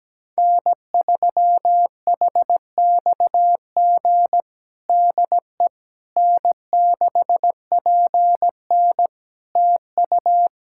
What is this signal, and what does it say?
Morse code, heard as a keyed tone, about 17 wpm, 700 Hz. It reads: N3HXG DE N6PN TU
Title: K1USN SST QSO Practice 17wpm